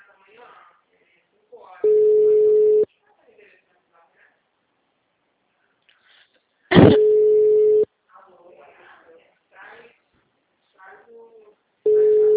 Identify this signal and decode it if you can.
Ringback tone (Europe, 425 Hz), with speech.
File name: out- -2011- - - .21502